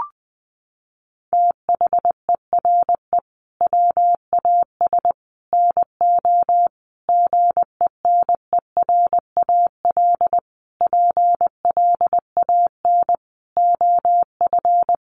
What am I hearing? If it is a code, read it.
Morse code, heard as a keyed tone, about 20 wpm, 700 Hz. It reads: THERE WAS NO GENERAL PLAN OF
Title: Sentences from Top 700 Words 20wpm